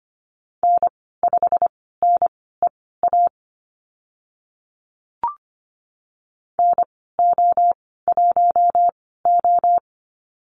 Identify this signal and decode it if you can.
Morse code, heard as a keyed tone, about 25 wpm, spaced as for 16 wpm, 700 Hz. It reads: N5NEA NO1O
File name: Lesson 15g - Number 5 - Call Signs 25wpm